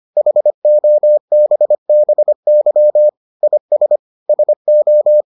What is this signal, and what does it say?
Morse code, heard as a keyed tone, about 25 wpm, 600 Hz. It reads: HOBBY IS SO